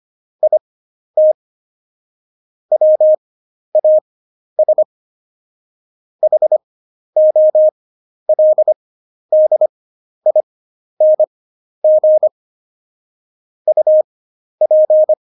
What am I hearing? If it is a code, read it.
Morse code, heard as a keyed tone, about 25 wpm, spaced as for 11 wpm, 620 Hz. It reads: IT WAS HOLDING UP
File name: Jokes-25-char-10-eff